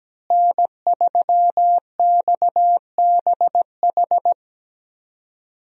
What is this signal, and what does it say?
Morse code, heard as a keyed tone, about 17 wpm, 700 Hz. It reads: N3XBH
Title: USA Call Signs - 5 Characters - Repeated 2x - 17wpm